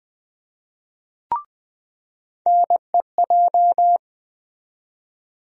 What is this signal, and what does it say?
Morse code, heard as a keyed tone, about 20 wpm, 700 Hz. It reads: NEJ
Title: USA Call Sign Postfixes - 3 Letters 20wpm